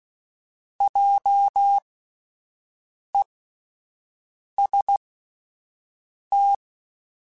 Morse code, heard as a keyed tone, about 16 wpm, spaced as for 5 wpm, 800 Hz. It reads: JEST
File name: Spaced Alphas / Spaced Alphas mp3s / Js sp @10